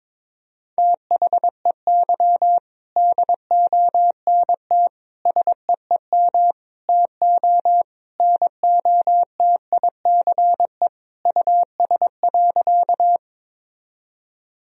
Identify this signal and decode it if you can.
Morse code, heard as a keyed tone, about 22 wpm, 700 Hz. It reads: THEY DONT SEEM TO NOTICE US.
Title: Sentences from Top 1000 Words - Review Part 2 - 22wpm